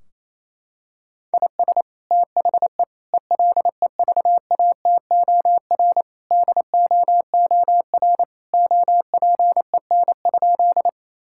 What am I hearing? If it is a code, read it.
Morse code, heard as a keyed tone, about 28 wpm, 700 Hz. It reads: IS THE ELEVATOR DOOR OPEN?
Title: Sentences from Top 1600 Words 28wpm